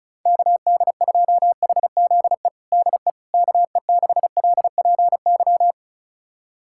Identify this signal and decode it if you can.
Morse code, heard as a keyed tone, about 36 wpm, 700 Hz. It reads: KD2HZE DE KE6LPY
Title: USA Callsign DE Callsign - 6 Characters - 35wpm